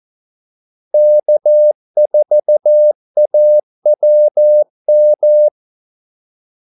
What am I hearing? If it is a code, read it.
Morse code, heard as a keyed tone, about 14 wpm, 600 Hz. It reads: K4AWM
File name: Callsigns-14wpm